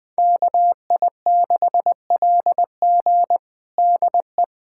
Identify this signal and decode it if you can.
Morse code, heard as a keyed tone, about 20 wpm, 700 Hz. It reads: KI6LG DE
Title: USA Callsign DE Callsign - 5 Characters - 20wpm